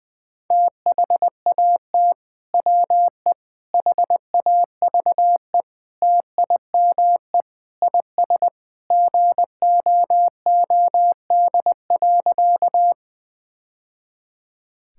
Morse code, heard as a keyed tone, about 20 wpm, 700 Hz. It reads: THAT WE HAVE TIME IS GOOD.
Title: Sentences with 3-Word Phrases - Spoken Beginning - 20wpm